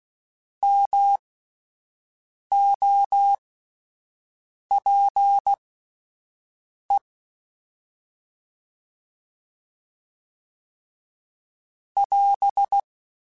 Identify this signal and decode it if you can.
Morse code, heard as a keyed tone, about 16 wpm, spaced as for 5 wpm, 800 Hz. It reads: MOPE <AS>